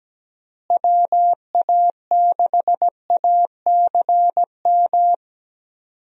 Morse code, heard as a keyed tone, about 17 wpm, 700 Hz. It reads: WA6ACM
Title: USA Call Signs - 6 Characters - Repeated 3x - 17wpm